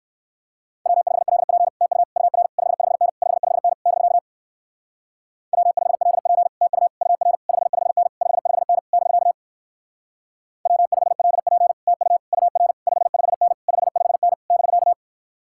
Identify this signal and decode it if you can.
Morse code, heard as a keyed tone, about 56 wpm, 700 Hz. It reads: W5LP TU UR 55N 55N <BK> W5LP TU UR 55N 55N <BK> W5LP TU UR 55N 55N <BK>
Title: Speed-Racing Common QSO Phrases - 45wpm